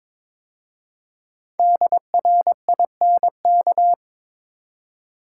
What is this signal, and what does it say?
Morse code, heard as a keyed tone, about 22 wpm, 700 Hz. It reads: DRINK
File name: 5 Letter Words - First 3 Letters Given - 22wpm